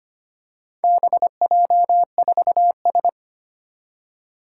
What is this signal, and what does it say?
Morse code, heard as a keyed tone, about 25 wpm, 700 Hz. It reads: BJ4S
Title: Four Challenging Letter-Number Combinations 25wpm